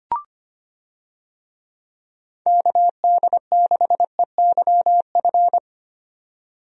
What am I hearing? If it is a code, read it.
Morse code, heard as a keyed tone, about 25 wpm, 700 Hz. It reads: KD6EYF
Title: USA Call Signs - 6 Characters - 25wpm